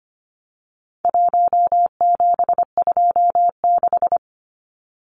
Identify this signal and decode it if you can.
Morse code, heard as a keyed tone, about 25 wpm, 700 Hz. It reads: 1726
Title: Numbers - 4-Digit 25wpm